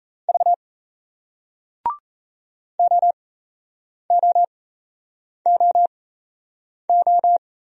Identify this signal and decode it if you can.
Morse code, heard as a keyed tone, about 38 wpm, spaced as for 8 wpm, 700 Hz. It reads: V OOOO